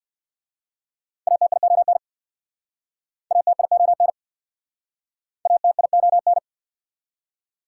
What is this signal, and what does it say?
Morse code, heard as a keyed tone, about 59 wpm, 690 Hz. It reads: ATION ATION ATION
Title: Speed-Racing Prefixes and Suffixes 50wpm